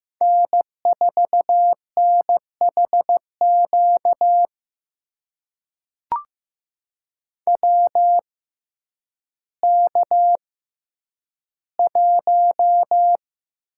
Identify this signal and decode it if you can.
Morse code, heard as a keyed tone, about 15 wpm, spaced as for 9 wpm, 700 Hz. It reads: N4NHQ WK1
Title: USA Call Signs - 6x Character Spacing - 15wpm